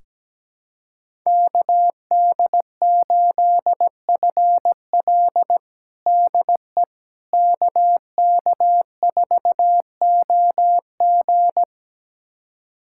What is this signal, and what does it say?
Morse code, heard as a keyed tone, about 17 wpm, 700 Hz. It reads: KD8FL DE KK4OG
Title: USA Callsign DE Callsign - 5 Characters - 17wpm